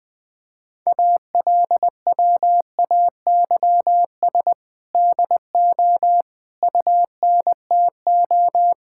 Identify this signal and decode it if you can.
Morse code, heard as a keyed tone, about 20 wpm, 700 Hz. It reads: ALWAYS DO UNTO
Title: Sentences with 8-Word Phrases - Spoken Beginning - 20wpm